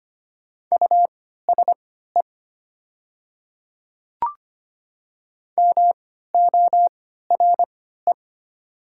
Morse code, heard as a keyed tone, about 25 wpm, spaced as for 14 wpm, 700 Hz. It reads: USE MORE
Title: 3x Character Spacing - Top 500 Words - 25wpm